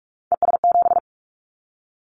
Morse code, heard as a keyed tone, about 46 wpm, 700 Hz. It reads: ES7
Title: Callsign Prefixes for Top 50 DXCC - 45wpm